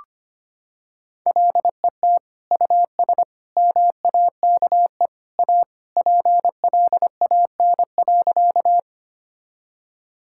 Morse code, heard as a keyed tone, about 25 wpm, 700 Hz. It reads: LET US MAKE A PLAN.
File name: practice / 25wpm / Sentences with 2-Word Phrases - 25wpm